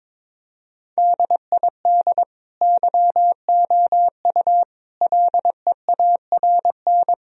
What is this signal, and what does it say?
Morse code, heard as a keyed tone, about 22 wpm, 700 Hz. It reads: DID YOU LEARN
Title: Sentences from Top 1000 Words - Review Part 2 - 22wpm